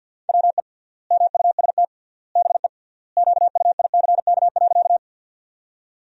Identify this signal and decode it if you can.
Morse code, heard as a keyed tone, about 50 wpm, 700 Hz. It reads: WE MUST BE QUICK.
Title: 3x Word-Spacing - Sentences from Top 500 Words - 50wpm